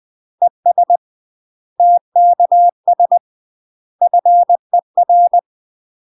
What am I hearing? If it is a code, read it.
Morse code, heard as a keyed tone, about 20 wpm, 700 Hz. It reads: ES TKS FER
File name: UK MASTER QSOs TRAINING / 20WPM / UK MASTER QSOs 0053